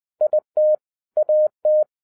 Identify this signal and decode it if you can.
Morse code, heard as a keyed tone, about 20 wpm, 610 Hz. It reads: IT AT